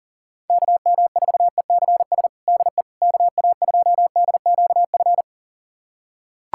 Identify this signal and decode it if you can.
Morse code, heard as a keyed tone, about 40 wpm, 700 Hz. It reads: KM4ECS DE KA2DQF
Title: USA Callsign DE Callsign - 6 Characters - 40wpm